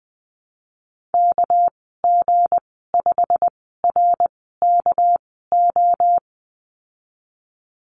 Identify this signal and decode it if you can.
Morse code, heard as a keyed tone, about 20 wpm, spaced as for 15 wpm, 700 Hz. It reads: KG5RKO
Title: 200 Lesson 32g - Letter k - Call Signs 20wpm